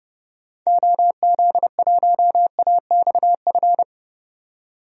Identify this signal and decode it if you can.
Morse code, heard as a keyed tone, about 30 wpm, 700 Hz. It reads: OZ1AXF
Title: DXCC-100 Call Signs - 6 Characters - 30wpm